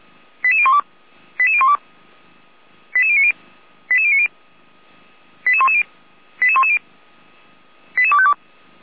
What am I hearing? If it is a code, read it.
Five-tone selective call (ZVEI), heard as quick runs of five tones, tones of about 70 ms. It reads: 88012 88012 88090 88090 88110 88110 88252